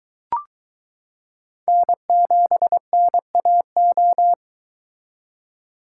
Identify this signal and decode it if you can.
Morse code, heard as a keyed tone, about 22 wpm, 700 Hz. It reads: N7NAO